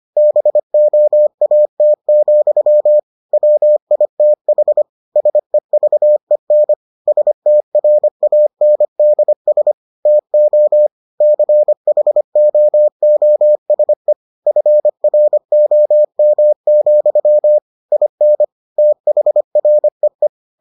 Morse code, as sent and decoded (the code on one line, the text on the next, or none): -... --- .- - --..-- / .-- .. - .... / ... ...- . -. / ... - .-. .- -. -.. ... / - --- / -.-. .... --- --- ... / ..-. .-. --- -- --..-- / .. -. / - .... .-. .
BOAT, WITH SEVEN STRANDS TO CHOOSE FROM, IN THREE